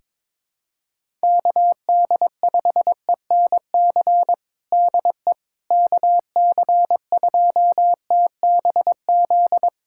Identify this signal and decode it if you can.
Morse code, heard as a keyed tone, about 22 wpm, 700 Hz. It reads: KD5ENC DE KC2TBZ
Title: USA Callsign DE Callsign - 6 Characters - 22wpm